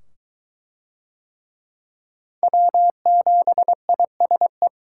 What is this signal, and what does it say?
Morse code, heard as a keyed tone, about 22 wpm, 700 Hz. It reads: W7ISE